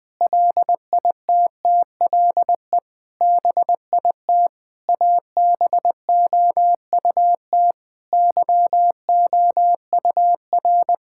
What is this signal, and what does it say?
Morse code, heard as a keyed tone, about 20 wpm, 700 Hz. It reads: LITTLE BIT ABOUT YOUR
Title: Sentences with 8-Word Phrases - 20wpm